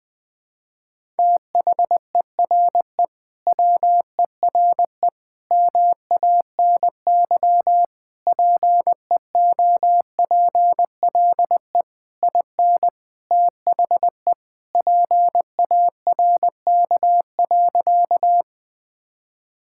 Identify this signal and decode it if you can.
Morse code, heard as a keyed tone, about 20 wpm, 700 Hz. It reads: THERE WERE MANY PEOPLE IN THE PARK.
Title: Sentences with 3-Word Phrases - 20wpm